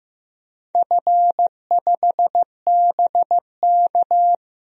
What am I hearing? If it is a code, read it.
Morse code, heard as a keyed tone, about 15 wpm, 700 Hz. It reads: F5BK
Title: DXCC-100 Call Signs - 4 Characters - 15wpm